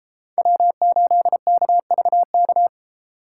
Morse code, heard as a keyed tone, about 32 wpm, 700 Hz. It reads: W8KVK